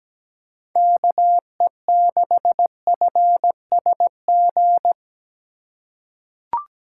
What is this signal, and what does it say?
Morse code, heard as a keyed tone, about 17 wpm, 700 Hz. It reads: KE6FSG
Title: USA Call Signs - 4x Character Spacing - 17wpm